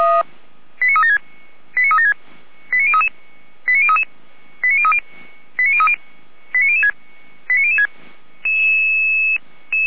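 Five-tone selective call (ZVEI), heard as quick runs of five tones, tones of about 70 ms. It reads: 89376 89376 89033 89033 89030 89030 89006 89006
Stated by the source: ZVEI five-tone selective call received via SDR, recorded by Feuerwehr Orlamünde fire brigade.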